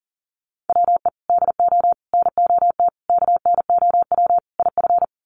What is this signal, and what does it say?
Morse code, heard as a keyed tone, about 40 wpm, 700 Hz. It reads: WE DO NOT KNOW IF
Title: Sets of 5 Words Encore2 40wpm